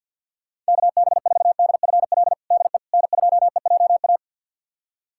Morse code, heard as a keyed tone, about 47 wpm, 700 Hz. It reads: KB4DFL DE N2EJA
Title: USA Callsign DE Callsign 50wpm